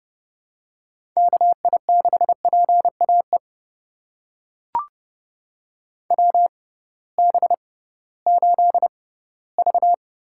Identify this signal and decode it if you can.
Morse code, heard as a keyed tone, about 30 wpm, spaced as for 17 wpm, 700 Hz. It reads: KI6PAE WB8V
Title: USA Call Signs - 6 Characters - 6x Character Spacing - 30wpm